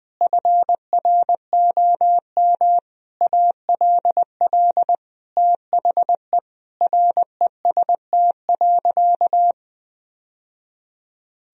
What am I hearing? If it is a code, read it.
Morse code, heard as a keyed tone, about 21 wpm, 700 Hz. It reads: FROM ALL THE REST.
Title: Sentences from Top 1000 Words - Review Part 1 - 20wpm